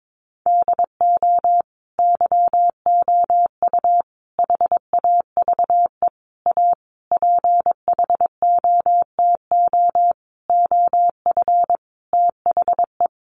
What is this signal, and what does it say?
Morse code, heard as a keyed tone, about 22 wpm, 700 Hz. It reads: DO YOU HAVE A PHOTO OF THE
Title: Sentences from Top 1200 Words 22wpm